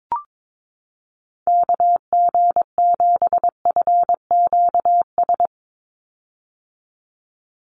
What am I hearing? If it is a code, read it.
Morse code, heard as a keyed tone, about 22 wpm, 700 Hz. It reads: KG7FQS
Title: USA Call Signs 22wpm